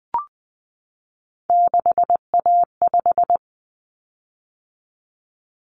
Morse code, heard as a keyed tone, about 20 wpm, 700 Hz. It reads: 6A5
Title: Common Cut Numbers - 3-Digit 20wpm